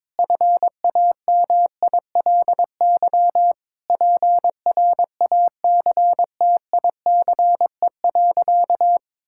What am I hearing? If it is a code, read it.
Morse code, heard as a keyed tone, about 22 wpm, 700 Hz. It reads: FAMILY PRACTICE.
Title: Sentences from Top 1000 Words - Review Part 5 - 22wpm